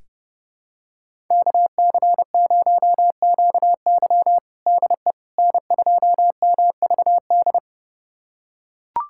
Morse code, heard as a keyed tone, about 30 wpm, 700 Hz. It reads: KC0QY DE N2MVD
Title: USA Callsign DE Callsign - 5 Characters - 30wpm